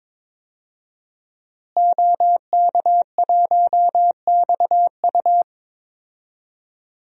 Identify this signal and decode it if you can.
Morse code, heard as a keyed tone, about 22 wpm, 700 Hz. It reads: OK1XU
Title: DXCC-100 Call Signs - 5 Characters - Repeated 2x - 22wpm